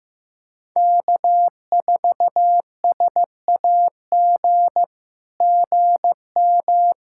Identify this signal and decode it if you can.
Morse code, heard as a keyed tone, about 15 wpm, 700 Hz. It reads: K4SAG GM
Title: SOTA QSO Practice 15wpm